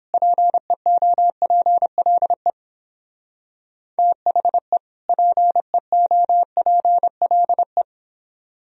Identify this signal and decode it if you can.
Morse code, heard as a keyed tone, about 29 wpm, 700 Hz. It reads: PEOPLE THE PEOPLE